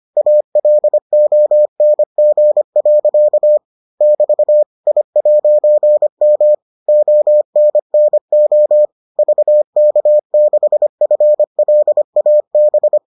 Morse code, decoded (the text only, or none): ALONG. <BT> I'M ONNO VK6FLAB